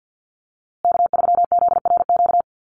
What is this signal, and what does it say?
Morse code, heard as a keyed tone, about 48 wpm, 700 Hz. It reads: K3ZRQ
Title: USA Call Signs Repeated 3x 50wpm